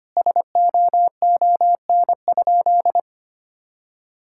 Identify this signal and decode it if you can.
Morse code, heard as a keyed tone, about 25 wpm, 700 Hz. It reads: SOON?
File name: practice / 25wpm / Sentences with 3-Word Phrases - 25wpm